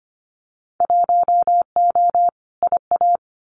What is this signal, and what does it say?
Morse code, heard as a keyed tone, about 25 wpm, 690 Hz. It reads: 1O IA
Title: Winter Field Day Exchanges Encore 25wpm